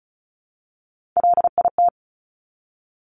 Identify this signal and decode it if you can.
Morse code, heard as a keyed tone, about 36 wpm, 700 Hz. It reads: LIT
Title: Top Three Letter Combinations in Words 35wpm